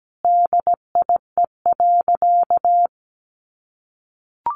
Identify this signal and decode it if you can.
Morse code, heard as a keyed tone, about 17 wpm, 700 Hz. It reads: DIE.
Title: Sentences from Top 1300 Words 17wpm